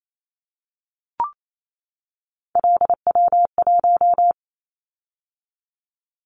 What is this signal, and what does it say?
Morse code, heard as a keyed tone, about 28 wpm, 700 Hz. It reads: LW1
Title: Callsign Prefixes for Top 50 DXCC - 28wpm